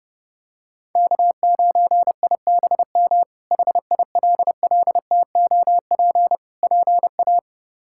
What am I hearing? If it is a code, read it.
Morse code, heard as a keyed tone, about 30 wpm, 700 Hz. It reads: K9IBM HILLTOP PA
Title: K1USN SST QSO Practice - 30wpm